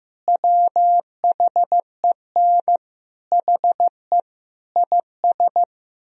Morse code, heard as a keyed tone, about 15 wpm, 700 Hz. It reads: WHEN HE IS
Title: Sentences from Top 1000 Words - Review Part 4 - 15wpm